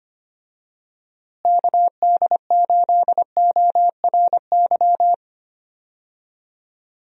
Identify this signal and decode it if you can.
Morse code, heard as a keyed tone, about 25 wpm, 700 Hz. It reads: KD8ORY
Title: USA Call Signs - 6 Characters - Repeated 2x - 25wpm